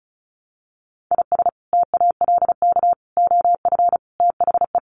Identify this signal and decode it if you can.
Morse code, heard as a keyed tone, about 34 wpm, 700 Hz. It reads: IS TALK OF THE
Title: Sets of 5 Words Encore2 35wpm